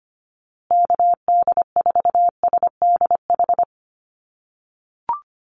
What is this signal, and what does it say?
Morse code, heard as a keyed tone, about 25 wpm, 700 Hz. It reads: KD4SDH
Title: USA Call Signs Repeated 2x 25wpm